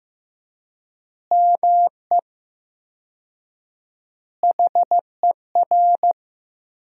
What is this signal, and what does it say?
Morse code, heard as a keyed tone, about 15 wpm, 700 Hz. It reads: ME HER
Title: 4x Word-Spacing - Sentences from Top 500 Words - 15wpm